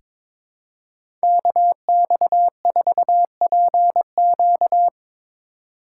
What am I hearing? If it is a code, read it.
Morse code, heard as a keyed tone, about 22 wpm, 700 Hz. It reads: KX4PQ
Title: K1USN SST QSO Practice - 22wpm